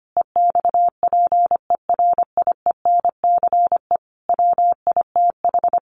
Morse code, heard as a keyed tone, about 25 wpm, 700 Hz. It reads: EXPERIENCE WITH
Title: Sentences from Top 600 Words 25wpm